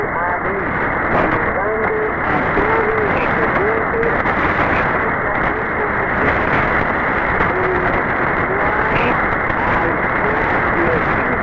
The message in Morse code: none